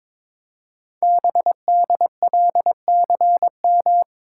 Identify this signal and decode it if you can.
Morse code, heard as a keyed tone, about 22 wpm, 700 Hz. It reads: BDLCM
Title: Code Groups - 5 Letters - 22wpm